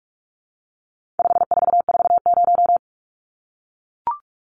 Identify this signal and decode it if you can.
Morse code, heard as a keyed tone, about 44 wpm, 700 Hz. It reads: 5440
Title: Numbers - 4-Digit 45wpm